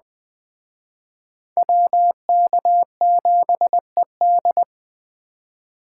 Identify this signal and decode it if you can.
Morse code, heard as a keyed tone, about 21 wpm, 700 Hz. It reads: WK7ED
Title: Speed Racing - USA 2x2 Call Signs 17wpm